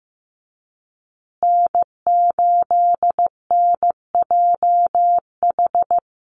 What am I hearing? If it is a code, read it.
Morse code, heard as a keyed tone, about 15 wpm, 700 Hz. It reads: N8NJH
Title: USA Call Signs Repeated 3x 15wpm